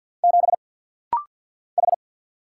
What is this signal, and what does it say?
Morse code, heard as a keyed tone, about 50 wpm, 700 Hz. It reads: 7 H
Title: Single Letter-Number - Rapid-Fire 50wpm